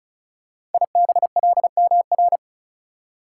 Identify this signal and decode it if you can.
Morse code, heard as a keyed tone, about 36 wpm, 690 Hz. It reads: IBLMR
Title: Code Groups - 5 Letters - 35wpm